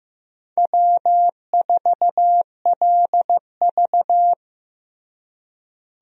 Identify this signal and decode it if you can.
Morse code, heard as a keyed tone, about 16 wpm, 700 Hz. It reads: W4LV
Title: USA Call Signs - 4 Characters - Repeated 2x - 15wpm